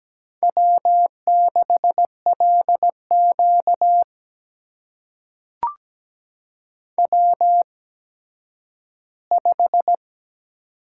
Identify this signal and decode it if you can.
Morse code, heard as a keyed tone, about 17 wpm, spaced as for 10 wpm, 700 Hz. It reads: W6LQ W5
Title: USA Call Signs - 4 Characters - 8x Character Spacing - 17wpm